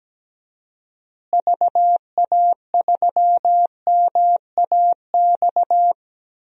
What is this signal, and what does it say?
Morse code, heard as a keyed tone, about 17 wpm, 700 Hz. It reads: VA3MAX
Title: DXCC-100 Call Signs - 6 Characters - Repeated 2x - 17wpm